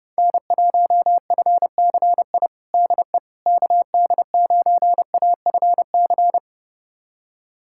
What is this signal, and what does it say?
Morse code, heard as a keyed tone, about 30 wpm, 690 Hz. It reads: N1FCI DE KD9AFC
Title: USA Callsign DE Callsign 30wpm